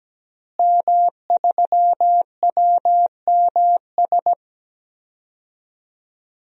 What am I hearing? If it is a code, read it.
Morse code, heard as a keyed tone, about 17 wpm, 700 Hz. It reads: M3WMS
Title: DXCC-100 Call Signs - 5 Characters - Repeated 3x - 17wpm